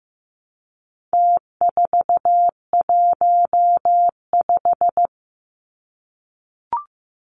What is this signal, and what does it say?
Morse code, heard as a keyed tone, about 15 wpm, 700 Hz. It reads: T415
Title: Numbers - Contest Style - 4-Digit 15wpm